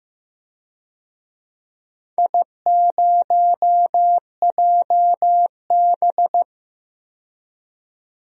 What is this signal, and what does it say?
Morse code, heard as a keyed tone, about 15 wpm, 700 Hz. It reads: I0JB
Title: DXCC-100 Call Signs - 4 Characters - 15wpm